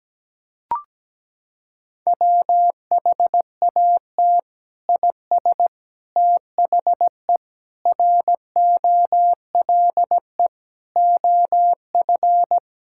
Morse code, heard as a keyed tone, about 17 wpm, 700 Hz. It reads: WHAT IS THE ROLE OF
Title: Sentences from Top 1000 Words - Review Part 6 - 17wpm